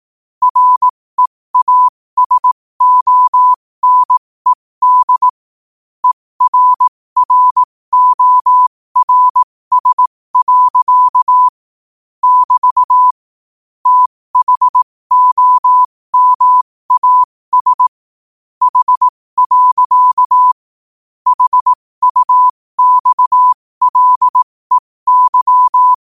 Morse code, heard as a keyed tone, about 18 wpm, spaced as for 15 wpm, 1000 Hz. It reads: REASONED ERRORS. <BT> THOMAS H. HUXLEY
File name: qotdfiles / 15wpm